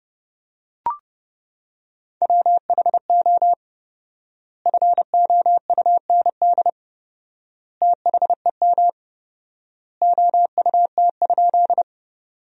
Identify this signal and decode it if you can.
Morse code, heard as a keyed tone, about 30 wpm, 700 Hz. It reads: WHO FOUND THEM OUT?
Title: 4x Word-Spacing - Sentences from Top 300 Words - 30wpm